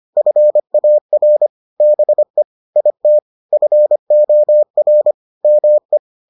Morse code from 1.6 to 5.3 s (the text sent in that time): BE IT FOR